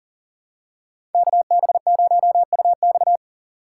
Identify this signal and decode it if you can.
Morse code, heard as a keyed tone, about 40 wpm, 700 Hz. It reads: KB0UX
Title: USA Call Signs - 5 Characters - 3x Character Spacing - 40wpm